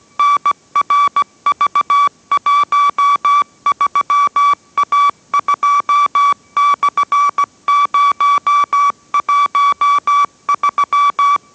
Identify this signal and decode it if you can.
Morse code, heard as a keyed tone, about 17 wpm, 1150 Hz. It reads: NRV13A2/013